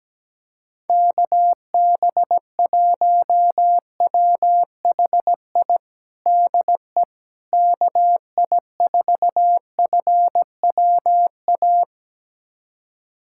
Morse code, heard as a keyed tone, about 17 wpm, 700 Hz. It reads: KB1WHI DE KI4FWA
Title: USA Callsign DE Callsign - 6 Characters - 17wpm